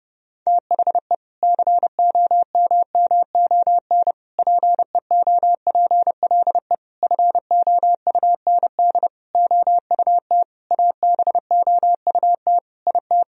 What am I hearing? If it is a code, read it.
Morse code, heard as a keyed tone, about 32 wpm, 700 Hz. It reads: THE COMMON PEOPLE FOUND OUT ABOUT IT
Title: Sentences from Top 1000 Words - Review Part 3 - 30wpm